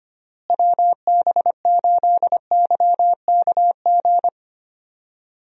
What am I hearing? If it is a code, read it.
Morse code, heard as a keyed tone, about 25 wpm, 700 Hz. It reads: WB8YKG
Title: USA Call Signs - 6 Characters - Repeated 2x - 25wpm